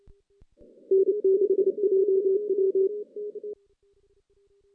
Morse code, heard as a keyed tone, about 27 wpm, 380 Hz. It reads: N6JW
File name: 2016 JAN CW NAQP / 2016 Jan naqp k5ot - 0-6